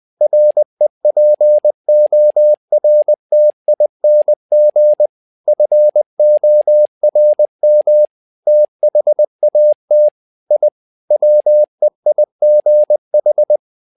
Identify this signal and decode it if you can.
Morse code, heard as a keyed tone, about 20 wpm, 600 Hz. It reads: REPORTING FORM THAT I WEIGH
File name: tower trouble 20wpm